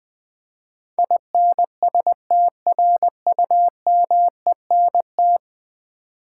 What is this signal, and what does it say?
Morse code, heard as a keyed tone, about 20 wpm, 700 Hz. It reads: INSTRUMENT
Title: Ten Letter Words - 20wpm